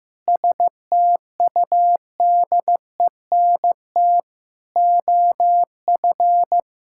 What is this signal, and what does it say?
Morse code, heard as a keyed tone, about 15 wpm, 700 Hz. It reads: STUDENT OF
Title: Sentences from Top 1000 Words - Review Part 6 - 15wpm